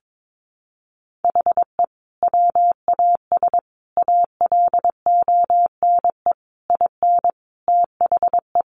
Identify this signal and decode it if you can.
Morse code, heard as a keyed tone, about 22 wpm, 700 Hz. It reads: HE WAS ALONE IN THE
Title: Sentences from Top 1000 Words 22wpm